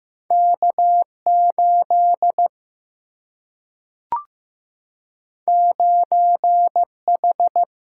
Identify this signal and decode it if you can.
Morse code, heard as a keyed tone, about 15 wpm, 700 Hz. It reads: K8 9H
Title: Single-Letter Single-Number 15wpm